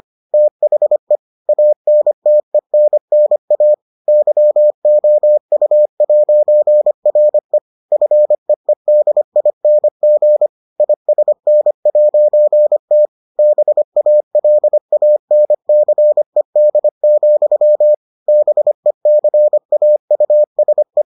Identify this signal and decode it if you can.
Morse code, heard as a keyed tone, about 25 wpm, 600 Hz. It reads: THE ANTENNA YOU'RE FEEDING ISN'T BALANCED, BECAUSE